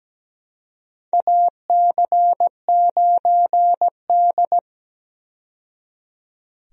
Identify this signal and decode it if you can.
Morse code, heard as a keyed tone, about 17 wpm, 700 Hz. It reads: AC9D
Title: USA Call Signs - 4 Characters - 17wpm